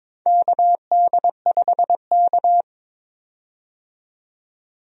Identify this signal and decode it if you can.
Morse code, heard as a keyed tone, about 21 wpm, 700 Hz. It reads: KD5K
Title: USA Call Signs - 4 Characters - 22wpm